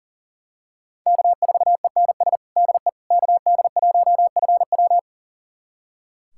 Morse code, heard as a keyed tone, about 37 wpm, 700 Hz. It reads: K4ENS DE KD1FW
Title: USA Callsign DE Callsign 40wpm